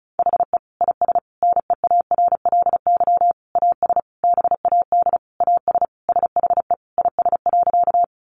Morse code, heard as a keyed tone, about 34 wpm, 700 Hz. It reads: HE IS NEARLY AS BAD AS SHE IS.
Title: Sentences from Top 1300 Words 35wpm